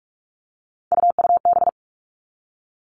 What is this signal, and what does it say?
Morse code, heard as a keyed tone, about 46 wpm, 700 Hz. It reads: UUB
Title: Three Challenging Letter-Number Combinations 45wpm